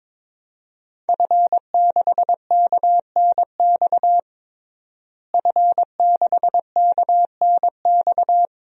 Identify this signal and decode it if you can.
Morse code, heard as a keyed tone, about 22 wpm, 700 Hz. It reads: F6KNX F6KNX
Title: DXCC-100 Call Signs - 5 Characters - Repeated 3x - 22wpm